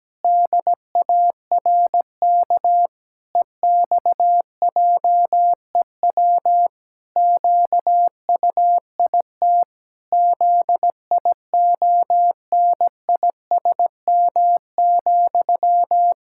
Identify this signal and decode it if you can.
Morse code, heard as a keyed tone, about 17 wpm, 700 Hz. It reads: DARK EXJEW QUIT ZIONISM,
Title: Pangram Challenge v2 - 17wpm